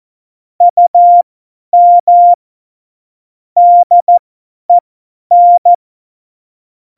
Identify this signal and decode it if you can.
Morse code, heard as a keyed tone, about 14 wpm, spaced as for 10 wpm, 700 Hz. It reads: UM DEN
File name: Morsekurs / Witze / Witz 17. November 2020, 50 F70